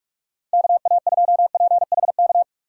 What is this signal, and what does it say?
Morse code, heard as a keyed tone, about 43 wpm, 700 Hz. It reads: KA2PHK
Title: USA Call Signs - 6 Characters - 6x Character Spacing - 45wpm